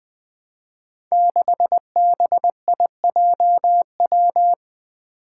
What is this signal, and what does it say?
Morse code, heard as a keyed tone, about 20 wpm, 700 Hz. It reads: 6BIJW
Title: Five Challenging Letter-Number Combinations 20wpm